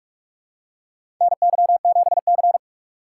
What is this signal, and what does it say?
Morse code, heard as a keyed tone, about 46 wpm, 700 Hz. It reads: NY7C
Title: USA Call Signs - 4 Characters - 6x Character Spacing - 45wpm